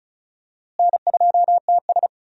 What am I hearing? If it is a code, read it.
Morse code, heard as a keyed tone, about 34 wpm, 700 Hz. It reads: N2TS
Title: USA Call Signs - 4 Characters - Repeated 3x - 35wpm